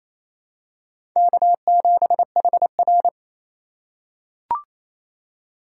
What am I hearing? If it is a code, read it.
Morse code, heard as a keyed tone, about 28 wpm, 700 Hz. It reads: K7HR
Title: USA Call Signs - 4 Characters - 3x Character Spacing - 28wpm